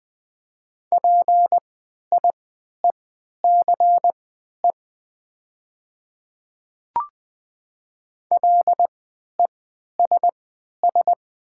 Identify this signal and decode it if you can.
Morse code, heard as a keyed tone, about 20 wpm, spaced as for 11 wpm, 700 Hz. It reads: PIECE LESS
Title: 3x Character Spacing - Top 500 Words - 20wpm